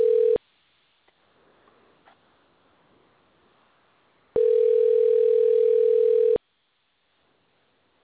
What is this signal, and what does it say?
Ringback tone (North America), 440 + 480 Hz.